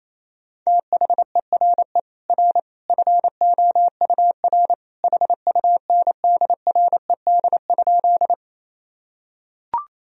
Morse code, heard as a keyed tone, about 28 wpm, 700 Hz. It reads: THERE R FOUR HUNDRED?
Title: Sentences from Top 400 Words and Abbreviations 28wpm